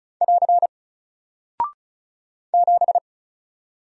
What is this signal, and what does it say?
Morse code, heard as a keyed tone, about 36 wpm, 700 Hz. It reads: <AR> 7
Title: Single Character (Letter:Number:Punctuation:Prosign) Alt 35wpm